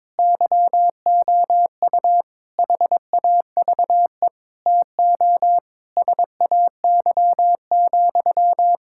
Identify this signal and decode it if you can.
Morse code, heard as a keyed tone, about 22 wpm, 700 Hz. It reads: YOU HAVE TO SAY,
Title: Sentences from Top 1600 Words 22wpm